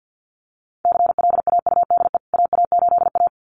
Morse code, heard as a keyed tone, about 48 wpm, 700 Hz. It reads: CLAUDE AA8A
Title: CWT Exchange 50wpm